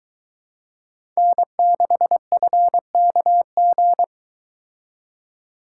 Morse code, heard as a keyed tone, about 22 wpm, 700 Hz. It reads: N6FKG